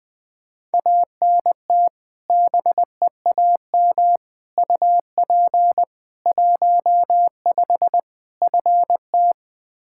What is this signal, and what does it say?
Morse code, heard as a keyed tone, about 21 wpm, 700 Hz. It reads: ANT BEAM UP 15 FT
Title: Practice QSO - Exchange 2 - Info - 20wpm